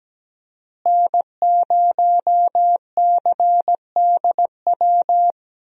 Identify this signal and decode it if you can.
Morse code, heard as a keyed tone, about 17 wpm, 700 Hz. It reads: N0CDW
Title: USA Call Signs - 5 Characters - Repeated 2x - 17wpm